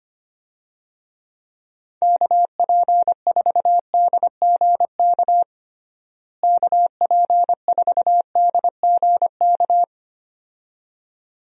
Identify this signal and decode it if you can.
Morse code, heard as a keyed tone, about 25 wpm, 700 Hz. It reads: KP4DGK KP4DGK
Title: USA Call Signs - 6 Characters - Repeated 2x - 25wpm